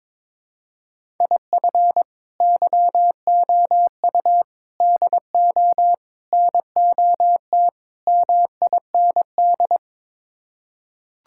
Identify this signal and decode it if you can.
Morse code, heard as a keyed tone, about 22 wpm, 700 Hz. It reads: IF YOU DO NOT MIND
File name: practice / 22wpm / Build A Phrase - 5-Word Phrases - 22wpm